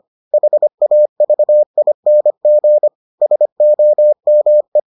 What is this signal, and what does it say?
Morse code, heard as a keyed tone, about 25 wpm, 600 Hz. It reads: HAVING SOME